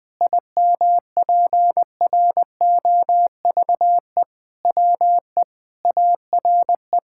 Morse code, heard as a keyed tone, about 20 wpm, 700 Hz. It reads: IMPROVE WE ARE
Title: Sentences with 6-Word Phrases - Spoken Beginning - 20wpm